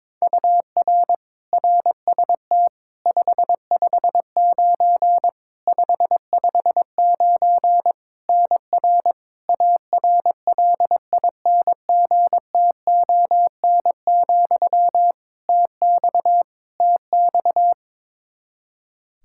Morse code, heard as a keyed tone, about 22 wpm, 700 Hz. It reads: UR RST 559 559 NR ARLINGTON, TX TX
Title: Practice QSO - Exchange 1 - Beginning - 22wpm